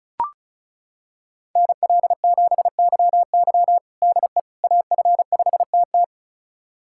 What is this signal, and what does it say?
Morse code, heard as a keyed tone, about 34 wpm, 700 Hz. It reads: NL7YY DE AF5TT
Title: USA Callsign DE Callsign - 5 Characters - 35wpm